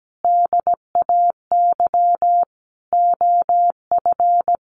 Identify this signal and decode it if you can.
Morse code, heard as a keyed tone, about 17 wpm, 700 Hz. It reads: DAY OF
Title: Sentences from Top 1100 Words 17wpm